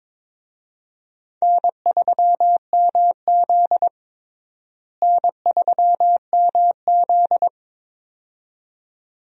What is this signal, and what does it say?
Morse code, heard as a keyed tone, about 22 wpm, 700 Hz. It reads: N3MZ N3MZ
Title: USA Call Signs - 4 Characters - Repeated 2x - 22wpm